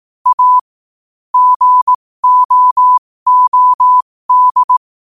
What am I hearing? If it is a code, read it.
Morse code, heard as a keyed tone, about 18 wpm, spaced as for 15 wpm, 1000 Hz. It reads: A GOOD
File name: qotdfiles / 15wpm